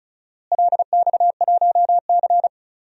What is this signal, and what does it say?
Morse code, heard as a keyed tone, about 34 wpm, 700 Hz. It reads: LX1C
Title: DXCC-100 Call Signs - 4 Characters - Repeated 3x - 35wpm